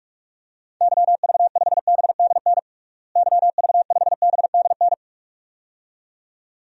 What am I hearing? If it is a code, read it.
Morse code, heard as a keyed tone, about 46 wpm, 700 Hz. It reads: YV5BDN YV5BDN
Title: DXCC-100 Call Signs - 6 Characters - Repeated 2x - 45wpm